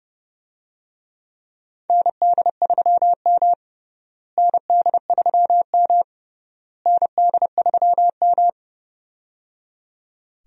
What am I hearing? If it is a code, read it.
Morse code, heard as a keyed tone, about 30 wpm, 700 Hz. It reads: ND3M ND3M ND3M
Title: USA Call Signs - 4 Characters - Repeated 3x - 30wpm